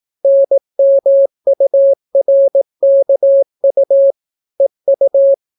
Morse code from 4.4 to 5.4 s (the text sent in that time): EU